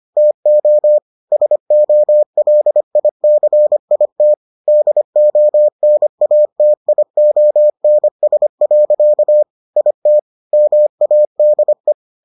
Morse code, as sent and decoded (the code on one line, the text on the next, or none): - --- / ... --- .-.. .. -.-. .. - / -.. --- -. .- - .. --- -. ... .-.-.- / .. - / -- .- -.. .
TO SOLICIT DONATIONS. IT MADE